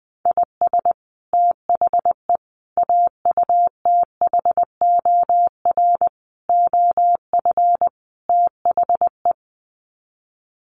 Morse code, heard as a keyed tone, about 20 wpm, 700 Hz. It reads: IS THE AUTHOR OF THE
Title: Sets of 5 Words Encore 20wpm